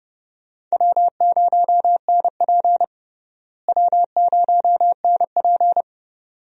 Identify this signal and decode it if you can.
Morse code, heard as a keyed tone, about 30 wpm, 700 Hz. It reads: W0NP W0NP
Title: USA Call Signs - 4 Characters - Repeated 2x - 30wpm